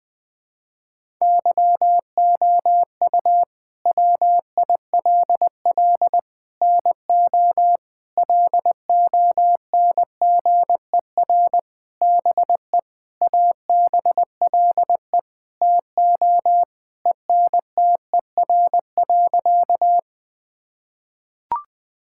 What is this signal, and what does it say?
Morse code, heard as a keyed tone, about 21 wpm, 700 Hz. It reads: YOU WILL NO LONGER BE ABLE TO ENTER.
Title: Sentences with 6-Word Phrases - 20wpm